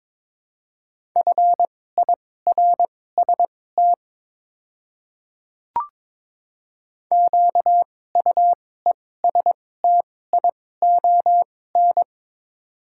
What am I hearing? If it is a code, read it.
Morse code, heard as a keyed tone, about 23 wpm, spaced as for 16 wpm, 700 Hz. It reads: FIRST QUESTION
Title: 2x Character Spacing - Top 500 Words - 22wpm